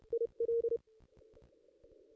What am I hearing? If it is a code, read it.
Morse code, heard as a keyed tone, about 30 wpm, 440 Hz. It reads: I L
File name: contests / ss-cw-k5tr-op-n5rz -1600